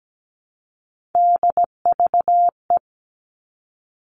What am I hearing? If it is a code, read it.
Morse code, heard as a keyed tone, about 17 wpm, 700 Hz. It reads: DVE
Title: All Cut Numbers - 3-Digit 17wpm